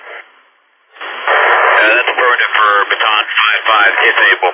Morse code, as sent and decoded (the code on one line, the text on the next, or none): none